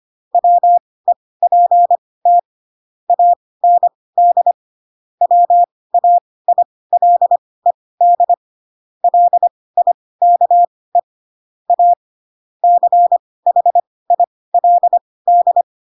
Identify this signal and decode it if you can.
Morse code, heard as a keyed tone, about 25 wpm, spaced as for 18 wpm, 700 Hz. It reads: WEPT AND WAILED LIKE A CHILD
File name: Pinocchio ch3 12 25 ik1hge